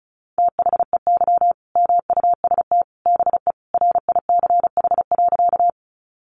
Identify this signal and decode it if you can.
Morse code, heard as a keyed tone, about 34 wpm, 700 Hz. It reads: THEY MUST BE RICH.
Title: Sentences from Top 1100 Words 35wpm